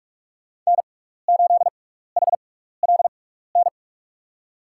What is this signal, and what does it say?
Morse code, heard as a keyed tone, about 46 wpm, spaced as for 15 wpm, 700 Hz. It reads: N8HLN
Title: USA Call Signs - 5 Characters - 6x Character Spacing - 45wpm